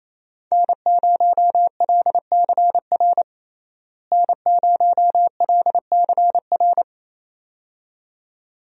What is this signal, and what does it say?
Morse code, heard as a keyed tone, about 28 wpm, 700 Hz. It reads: N0LCR N0LCR
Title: USA Call Signs - 5 Characters - Repeated 2x - 28wpm